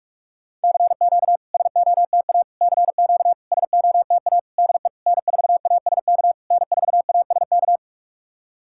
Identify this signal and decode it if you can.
Morse code, heard as a keyed tone, about 46 wpm, 690 Hz. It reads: CQ SOTA CQ SOTA DE N4ASK N4ASK
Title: SOTA QSO Practice 45wpm